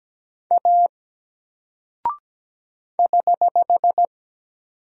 Morse code, heard as a keyed tone, about 17 wpm, 700 Hz. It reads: A <HH>
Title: Single Character (Letter:Number:Punctuation:Prosign) 17wpm